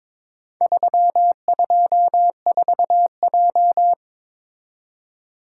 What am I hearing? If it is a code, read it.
Morse code, heard as a keyed tone, about 22 wpm, 690 Hz. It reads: 324J
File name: practice / 22wpm / Four Letter-Number Combinations - 22wpm